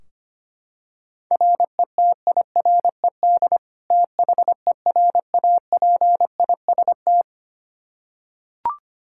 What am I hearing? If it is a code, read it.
Morse code, heard as a keyed tone, about 25 wpm, 700 Hz. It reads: RETIRED THERAPIST
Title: Practice QSO - Exchange 3 - More Info - 25wpm